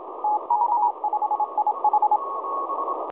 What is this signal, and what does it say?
Morse code, heard as a keyed tone, about 26 wpm, 840 Hz, with radio noise. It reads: TK5IH
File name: 4U2STAYHOME 7016.2kHz